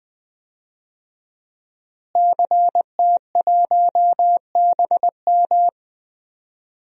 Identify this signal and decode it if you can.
Morse code, heard as a keyed tone, about 20 wpm, 700 Hz. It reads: CT1BM
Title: DXCC-100 Call Signs - 5 Characters - Repeated 2x - 20wpm